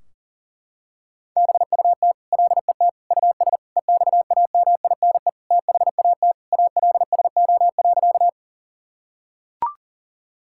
Morse code, heard as a keyed tone, about 40 wpm, 700 Hz. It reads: BUT LET US EXAMINE THAT ALSO.
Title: Sentences from Top 1600 Words 40wpm